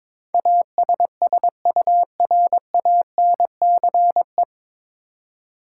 Morse code, heard as a keyed tone, about 22 wpm, 700 Hz. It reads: ASSURANCE
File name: 9 Letter Words - First 3 Letters Given - 22wpm